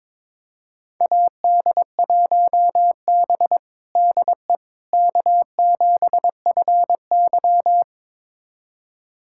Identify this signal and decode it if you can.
Morse code, heard as a keyed tone, about 22 wpm, 700 Hz. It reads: AD1B DE K7FY
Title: USA Callsign DE Callsign - 4 Characters - 22wpm